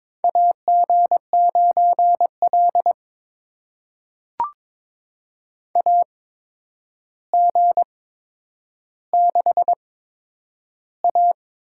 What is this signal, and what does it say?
Morse code, heard as a keyed tone, about 22 wpm, spaced as for 9 wpm, 700 Hz. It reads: AG9L AG6A
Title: USA Call Signs - 4 Characters - 8x Character Spacing - 22wpm